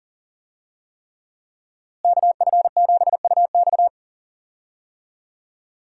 Morse code, heard as a keyed tone, about 40 wpm, 700 Hz. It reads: KF7UX
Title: USA Call Signs - 5 Characters - Repeated 3x - 40wpm